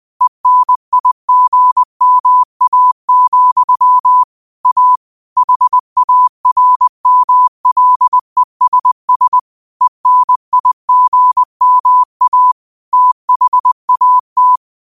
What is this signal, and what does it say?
Morse code, heard as a keyed tone, about 20 wpm, 1000 Hz. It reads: ENIGMA, A HARMLESS ENIGMA THAT
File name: qotdfiles / 20wpm